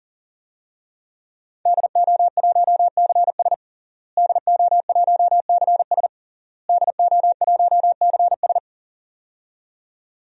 Morse code, heard as a keyed tone, about 40 wpm, 700 Hz. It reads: DO1CS DO1CS DO1CS
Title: DXCC-100 Call Signs - 5 Characters - Repeated 3x - 40wpm